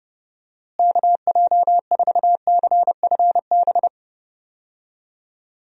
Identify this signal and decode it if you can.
Morse code, heard as a keyed tone, about 30 wpm, 700 Hz. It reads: KJ4CFB
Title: USA Call Signs - 6 Characters - Repeated 2x - 30wpm